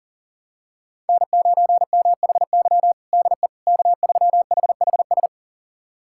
Morse code, heard as a keyed tone, about 40 wpm, 700 Hz. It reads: N9MHY DE K3HHS
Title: USA Callsign DE Callsign - 5 Characters - 40wpm